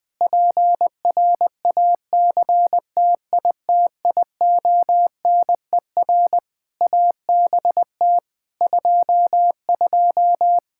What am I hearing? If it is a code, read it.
Morse code, heard as a keyed tone, about 20 wpm, 700 Hz. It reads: PRACTITIONER ABT 22
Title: Practice QSO - Exchange 3 - More Info - 20wpm